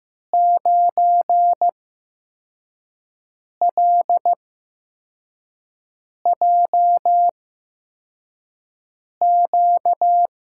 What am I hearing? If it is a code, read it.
Morse code, heard as a keyed tone, about 15 wpm, spaced as for 4 wpm, 700 Hz. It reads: 9LJQ
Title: USA Call Signs - 5 Characters - 8x Character Spacing - 15wpm